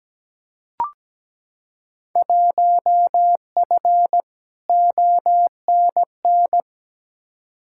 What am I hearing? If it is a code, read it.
Morse code, heard as a keyed tone, about 17 wpm, 700 Hz. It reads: 1F ONN
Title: Field Day Exchanges Encore - 17wpm